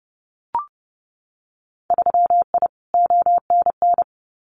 Morse code, heard as a keyed tone, about 30 wpm, 700 Hz. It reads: 3I ONN
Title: Winter Field Day Exchanges Encore 30wpm